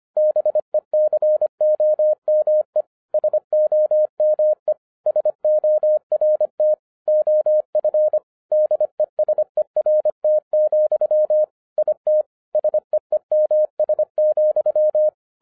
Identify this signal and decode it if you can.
Morse code, heard as a keyed tone, about 25 wpm, 610 Hz. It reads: BECOME SOME SORT OF DESERT, IT SEEMS,